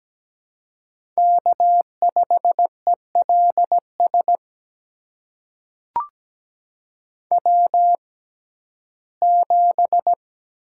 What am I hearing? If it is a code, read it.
Morse code, heard as a keyed tone, about 17 wpm, spaced as for 12 wpm, 700 Hz. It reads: K5ELS W7
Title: USA Call Signs - 5 Characters - 6x Character Spacing - 17wpm